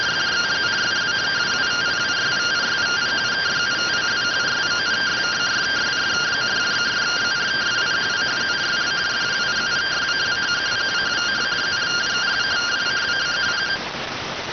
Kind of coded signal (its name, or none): two-tone teleprinter signal